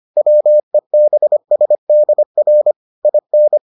Morse code, heard as a keyed tone, about 25 wpm, 600 Hz. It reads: WEBSDR IN